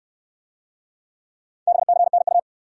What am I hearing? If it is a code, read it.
Morse code, heard as a keyed tone, about 62 wpm, 700 Hz. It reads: BCNU